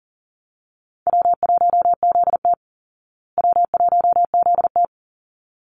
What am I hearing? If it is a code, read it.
Morse code, heard as a keyed tone, about 40 wpm, 700 Hz. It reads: W1ZT W1ZT
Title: Easy Call Signs Repeated 2x - 40wpm